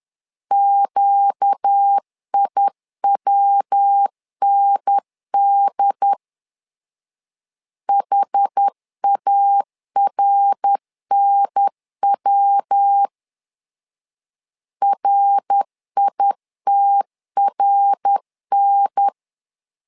Morse code, heard as a keyed tone, about 10 wpm, 800 Hz. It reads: QIWND HARNW RITRN
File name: LEZIONI / 4 Q 10wpm 2210